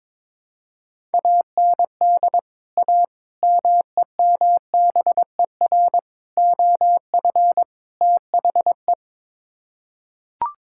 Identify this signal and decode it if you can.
Morse code, heard as a keyed tone, about 22 wpm, 700 Hz. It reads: AND A MEMBER OF THE
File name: practice / 22wpm / Build A Phrase - 5-Word Phrases - 22wpm